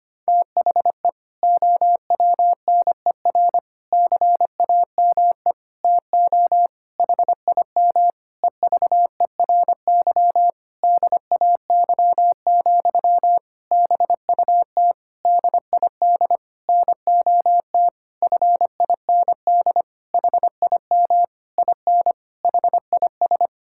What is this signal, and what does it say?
Morse code, heard as a keyed tone, about 25 wpm, 700 Hz. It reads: THE OWNER CAME TO HIM EVERY DAY, BUT DID NOT FIND HIM IN HIS
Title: Sentences from Top 1600 Words 25wpm